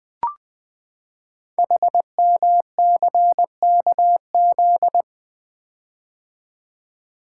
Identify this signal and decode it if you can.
Morse code, heard as a keyed tone, about 21 wpm, 700 Hz. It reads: HMCKZ
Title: Code Groups - 5 Letters - 20wpm